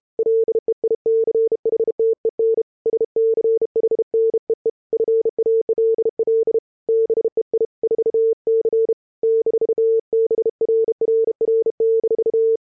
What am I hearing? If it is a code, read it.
Morse code, heard as a keyed tone, about 30 wpm, 440 Hz. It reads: LEICHTEN SCHNEEFALL BEI 4C -BRRR-